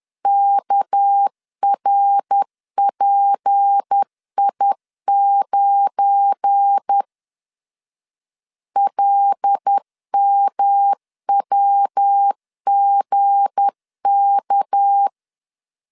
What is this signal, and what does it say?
Morse code, heard as a keyed tone, about 10 wpm, 800 Hz. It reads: KRPI9 LMWGK
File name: LEZIONI / 8 YJ45 / 8 Y 10wpm 2615